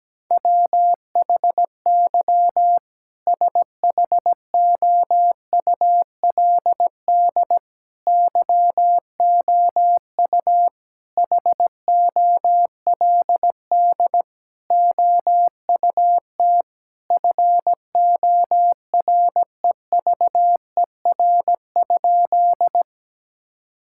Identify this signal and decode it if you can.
Morse code, heard as a keyed tone, about 17 wpm, 700 Hz. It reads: WHY SHOULD YOU HOLD OUT FOREVER?
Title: Sentences from Top 1600 Words 17wpm